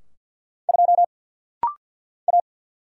Morse code, heard as a keyed tone, about 50 wpm, 700 Hz. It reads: 2 A
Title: Single Letter-Number - Rapid-Fire 50wpm